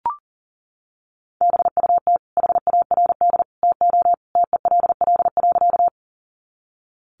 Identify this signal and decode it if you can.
Morse code, heard as a keyed tone, about 40 wpm, 700 Hz. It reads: BUT HARD TO TELL.
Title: Sentences from Top 300 Words 40wpm